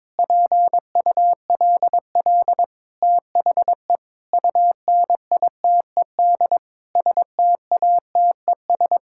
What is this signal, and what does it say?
Morse code, heard as a keyed tone, about 22 wpm, 700 Hz. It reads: PULL THE UNITED STATES
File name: practice / 22wpm / Sentences with 8-Word Phrases - 22wpm